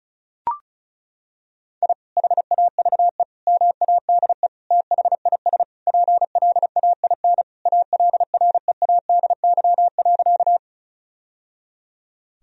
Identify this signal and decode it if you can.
Morse code, heard as a keyed tone, about 36 wpm, 700 Hz. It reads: I HAVE MADE THIS PLAIN ALREADY.
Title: Sentences from Top 1000 Words - Review Part 2 - 35wpm